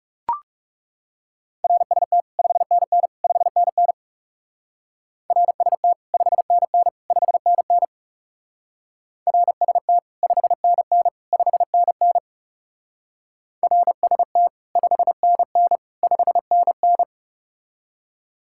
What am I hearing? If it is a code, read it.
Morse code, heard as a keyed tone, about 40 wpm, 700 Hz. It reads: RST 5NN 5NN RST 5NN 5NN RST 5NN 5NN RST 5NN 5NN